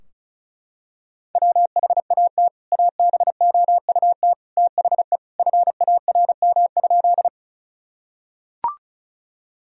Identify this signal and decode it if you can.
Morse code, heard as a keyed tone, about 34 wpm, 700 Hz. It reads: WHAT ABOUT THE FARM?
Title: Sentences from Top 1000 Words - Review Part 2 - 35wpm